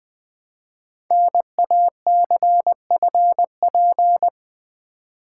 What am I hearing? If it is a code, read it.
Morse code, heard as a keyed tone, about 20 wpm, 700 Hz. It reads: NACFP
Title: Code Groups - 5 Letter:Numbers - 20wpm